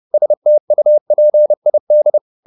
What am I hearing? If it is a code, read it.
Morse code, heard as a keyed tone, about 30 wpm, 600 Hz. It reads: STUPID
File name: the old man and the sea / the old man and the sea 1 30wpm